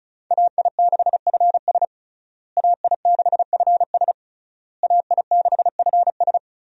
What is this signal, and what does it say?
Morse code, heard as a keyed tone, about 36 wpm, 700 Hz. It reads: AI6FS AI6FS AI6FS
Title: USA Call Signs - 5 Characters - Repeated 3x - 35wpm